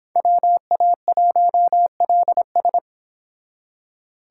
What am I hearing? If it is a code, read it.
Morse code, heard as a keyed tone, about 26 wpm, 700 Hz. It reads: WA1LS